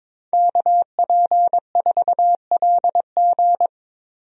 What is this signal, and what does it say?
Morse code, heard as a keyed tone, about 21 wpm, 700 Hz. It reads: KP4LG
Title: DXCC-100 Call Signs - 5 Characters - Repeated 2x - 22wpm